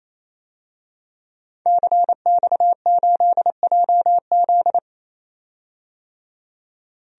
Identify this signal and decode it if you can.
Morse code, heard as a keyed tone, about 28 wpm, 700 Hz. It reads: CX8JZ
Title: DXCC-100 Call Signs - 5 Characters - 28wpm